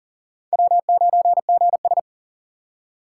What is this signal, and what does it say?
Morse code, heard as a keyed tone, about 53 wpm, 700 Hz. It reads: W9GS